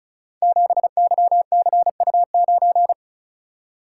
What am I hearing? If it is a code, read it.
Morse code, heard as a keyed tone, about 36 wpm, 700 Hz. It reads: 7YCU9
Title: Code Groups - 5 Letter:Numbers - 35wpm